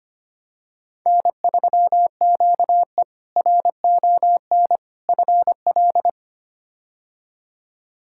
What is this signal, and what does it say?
Morse code, heard as a keyed tone, about 25 wpm, 700 Hz. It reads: N3QE RON FL
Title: K1USN SST QSO Practice - 25wpm